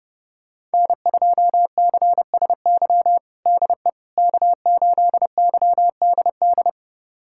Morse code, heard as a keyed tone, about 30 wpm, 700 Hz. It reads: N2CSY DE K8YDD
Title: USA Callsign DE Callsign - 5 Characters - 30wpm